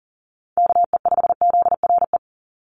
Morse code, heard as a keyed tone, about 40 wpm, 700 Hz. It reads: KE5ZRE
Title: USA Call Signs 40wpm